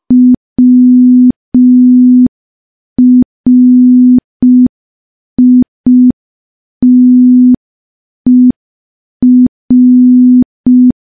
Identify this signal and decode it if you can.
Morse code, heard as a keyed tone, about 5 wpm, 250 Hz. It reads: WRITER
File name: morse / morse-training-05WPM